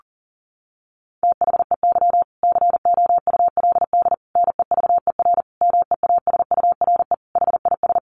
Morse code, heard as a keyed tone, about 40 wpm, 700 Hz. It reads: THEY COULD NEVER MEASURE HIS
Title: Sentences from Top 1200 Words 40wpm